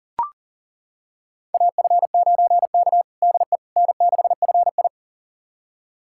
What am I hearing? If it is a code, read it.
Morse code, heard as a keyed tone, about 40 wpm, 700 Hz. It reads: AF9K DE N6FI
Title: USA Callsign DE Callsign - 4 Characters - 40wpm